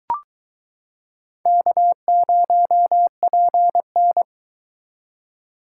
Morse code, heard as a keyed tone, about 22 wpm, 700 Hz. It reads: K0PN